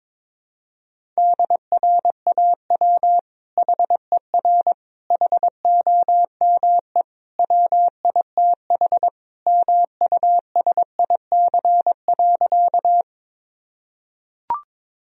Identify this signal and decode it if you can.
Morse code, heard as a keyed tone, about 22 wpm, 700 Hz. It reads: DRAW HER HOME WITH MUSIC.
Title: Sentences from Top 1000 Words - Review Part 1 - 22wpm